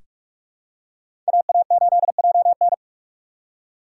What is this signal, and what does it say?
Morse code, heard as a keyed tone, about 46 wpm, 700 Hz. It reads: AA8JN
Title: USA Call Signs - 5 Characters - 2x Character Spacing - 45wpm